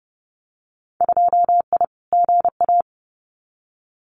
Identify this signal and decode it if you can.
Morse code, heard as a keyed tone, about 30 wpm, 700 Hz. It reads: 2I GA
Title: Winter Field Day Exchanges 30wpm